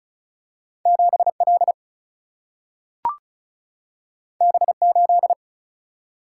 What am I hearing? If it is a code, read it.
Morse code, heard as a keyed tone, about 36 wpm, 700 Hz. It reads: 7L B8
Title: Single-Letter Single-Number 35wpm